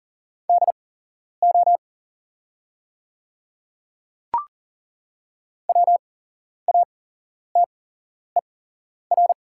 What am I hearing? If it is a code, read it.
Morse code, heard as a keyed tone, about 40 wpm, spaced as for 11 wpm, 700 Hz. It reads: DO WATER
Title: 8x Character Spacing - Top 200 Words - 40wpm